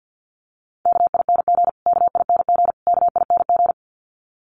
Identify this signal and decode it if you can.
Morse code, heard as a keyed tone, about 54 wpm, 700 Hz. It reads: KING KING KING
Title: Top 500 Words Repeated 50wpm